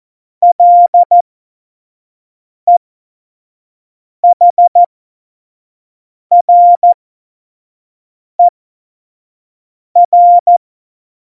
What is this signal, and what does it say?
Morse code, heard as a keyed tone, about 14 wpm, spaced as for 5 wpm, 700 Hz. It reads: LEHRER